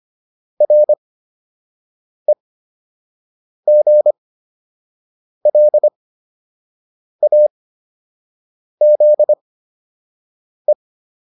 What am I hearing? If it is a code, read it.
Morse code, heard as a keyed tone, about 25 wpm, spaced as for 6 wpm, 600 Hz. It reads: REGLAZE
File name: RE7 06